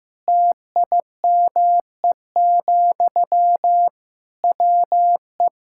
Morse code, heard as a keyed tone, about 15 wpm, 700 Hz. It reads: TIME, WE
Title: Sentences from Top 1000 Words - Review Part 6 - 15wpm